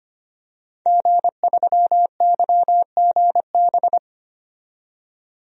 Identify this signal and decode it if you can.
Morse code, heard as a keyed tone, about 25 wpm, 700 Hz. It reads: G3YGB
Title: DXCC-100 Call Signs - 5 Characters - 25wpm